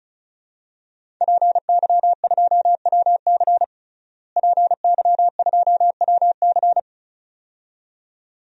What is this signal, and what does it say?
Morse code, heard as a keyed tone, about 36 wpm, 700 Hz. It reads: PY2WC PY2WC
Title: DXCC-100 Call Signs - 5 Characters - Repeated 2x - 35wpm